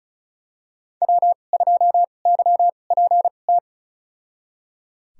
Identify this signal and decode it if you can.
Morse code, heard as a keyed tone, about 36 wpm, spaced as for 25 wpm, 700 Hz. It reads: W2YPT
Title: USA Call Signs - 5 Characters - 2x Character Spacing - 35wpm